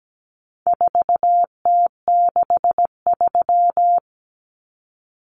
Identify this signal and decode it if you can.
Morse code, heard as a keyed tone, about 17 wpm, 700 Hz. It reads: 4T63
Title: Common Cut Numbers - 4-Digit 17wpm